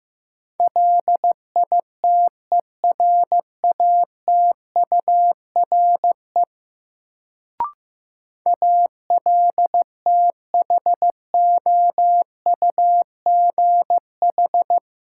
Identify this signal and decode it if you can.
Morse code, heard as a keyed tone, about 15 wpm, 700 Hz. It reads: LITERATURE ALTHOUGH
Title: Sentences from Top 600 Words - New Word Drill - 15wpm